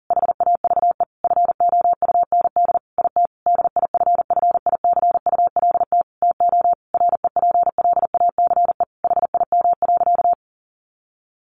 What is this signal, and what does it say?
Morse code, heard as a keyed tone, about 40 wpm, 700 Hz. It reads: HAVE FOUND IT DIFFICULT TO REPLACE HIM.
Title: Sentences from Top 1400 Words 40wpm